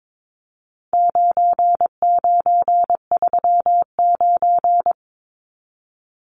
Morse code, heard as a keyed tone, about 22 wpm, 700 Hz. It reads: 9939
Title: Numbers - 4-Digit 22wpm